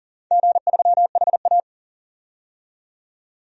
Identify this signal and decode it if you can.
Morse code, heard as a keyed tone, about 39 wpm, 700 Hz. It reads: G3HA
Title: DXCC-100 Call Signs - 4 Characters - 40wpm